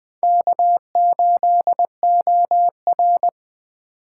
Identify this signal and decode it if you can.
Morse code, heard as a keyed tone, about 21 wpm, 700 Hz. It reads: K8OR